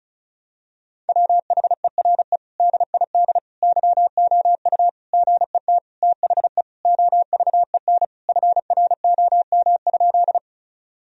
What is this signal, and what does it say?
Morse code, heard as a keyed tone, about 34 wpm, 700 Hz. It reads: WHERE DID YOU GET THE OVEN FROM?
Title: Sentences from Top 1000 Words - Review Part 4 - 35wpm